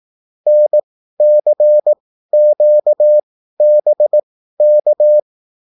Morse code, heard as a keyed tone, about 18 wpm, spaced as for 13 wpm, 600 Hz. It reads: NCQBK
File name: QSL / CW lesson